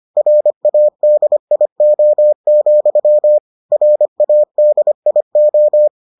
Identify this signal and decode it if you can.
Morse code, heard as a keyed tone, about 25 wpm, 600 Hz. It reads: RADIO, RADIO